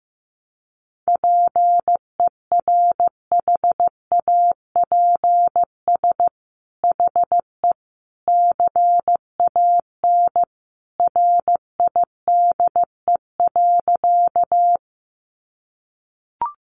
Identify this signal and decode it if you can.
Morse code, heard as a keyed tone, about 15 wpm, 700 Hz. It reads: PERHAPS HE CAN RIDE.
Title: Sentences from Top 1200 Words 15wpm